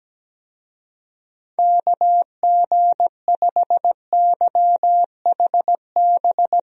Morse code, heard as a keyed tone, about 17 wpm, 700 Hz. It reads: KG5YHB
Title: USA Call Signs - 6 Characters - Repeated 2x - 17wpm